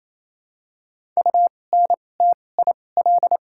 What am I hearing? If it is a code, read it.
Morse code, heard as a keyed tone, about 28 wpm, spaced as for 20 wpm, 700 Hz. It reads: UNTIL
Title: 2x Character Spacing - Top 500 Words - 28wpm